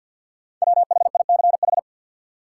Morse code, heard as a keyed tone, about 53 wpm, 700 Hz. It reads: WHICH
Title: Sentences from Top 100 Words - New Word Drill - 50wpm